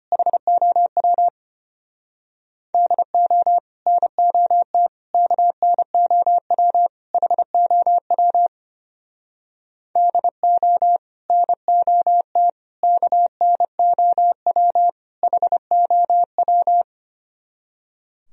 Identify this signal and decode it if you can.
Morse code, heard as a keyed tone, about 31 wpm, 700 Hz. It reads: HOW DO NOT KNOW HOW DO NOT KNOW HOW
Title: Speed-Racing - Sets of 4 Words 25wpm